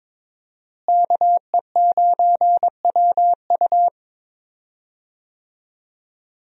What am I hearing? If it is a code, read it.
Morse code, heard as a keyed tone, about 22 wpm, 700 Hz. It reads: KE9WU
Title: USA Call Signs - 5 Characters - Repeated 2x - 22wpm